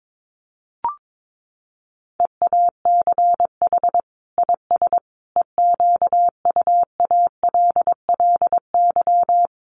Morse code, heard as a keyed tone, about 22 wpm, 700 Hz. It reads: EACH IS EQUALLY
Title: Sentences from Top 1400 Words 22wpm